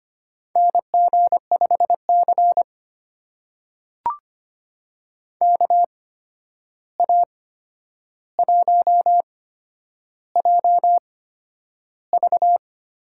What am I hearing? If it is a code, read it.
Morse code, heard as a keyed tone, about 25 wpm, spaced as for 10 wpm, 700 Hz. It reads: NG5C KA1JV
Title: USA Call Signs - 8x Character Spacing - 25wpm